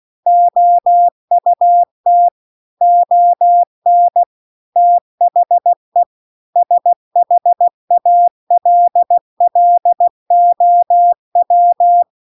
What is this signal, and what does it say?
Morse code, heard as a keyed tone, about 16 wpm, 700 Hz. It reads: OUT ON THE SHALLOW
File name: CW-The-Big-Twoheaded-River-2-16WPM